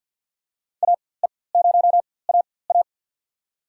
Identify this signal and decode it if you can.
Morse code, heard as a keyed tone, about 52 wpm, spaced as for 24 wpm, 700 Hz. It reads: AE0AA
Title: USA Call Signs - 5 Characters - 4x Character Spacing - 50wpm